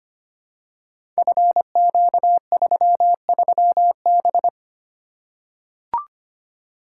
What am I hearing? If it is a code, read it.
Morse code, heard as a keyed tone, about 25 wpm, 700 Hz. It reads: FQ33B
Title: Code Groups - 5 Letter:Numbers - 25wpm